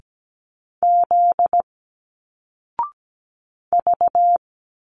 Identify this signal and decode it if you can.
Morse code, heard as a keyed tone, about 17 wpm, 700 Hz. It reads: Z V
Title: Single Letters 17wpm